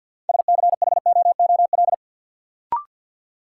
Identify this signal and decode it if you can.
Morse code, heard as a keyed tone, about 52 wpm, 700 Hz. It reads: SCHOOL
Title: Sets of 2 Words - New Word Drill - 50wpm